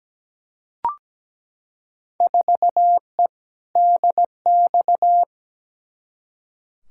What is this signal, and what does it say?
Morse code, heard as a keyed tone, about 17 wpm, 700 Hz. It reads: 4E DX
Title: Field Day Exchanges Encore - 17wpm